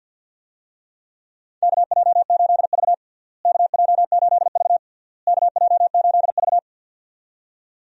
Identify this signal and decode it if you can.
Morse code, heard as a keyed tone, about 53 wpm, 700 Hz. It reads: KJ8V KJ8V KJ8V
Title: USA Call Signs - 4 Characters - Repeated 3x - 50wpm